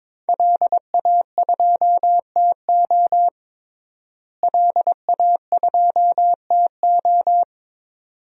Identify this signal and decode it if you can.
Morse code, heard as a keyed tone, about 22 wpm, 700 Hz. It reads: LA2TO LA2TO
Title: DXCC-100 Call Signs - 5 Characters - Repeated 2x - 22wpm